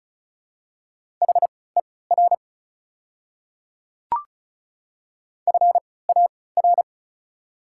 Morse code, heard as a keyed tone, about 34 wpm, spaced as for 20 wpm, 700 Hz. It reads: HER FAR
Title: 3x Character Spacing - Top 500 Words - 35wpm